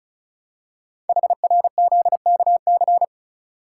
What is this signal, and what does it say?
Morse code, heard as a keyed tone, about 36 wpm, 690 Hz. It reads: HRZKC
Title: Code Groups - 5 Letters - 35wpm